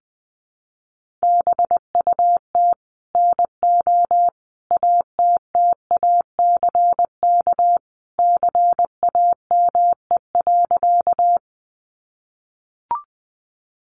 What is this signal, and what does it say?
Morse code, heard as a keyed tone, about 20 wpm, 700 Hz. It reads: BUT NO ATTACK CAME.
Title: Sentences from Top 1500 Words 20wpm